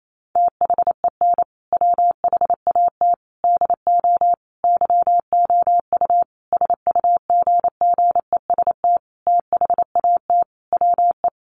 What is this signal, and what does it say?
Morse code, heard as a keyed tone, about 28 wpm, 700 Hz. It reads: THEN WHAT DO YOU SUGGEST THAT WE
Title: Sentences from Top 1300 Words 28wpm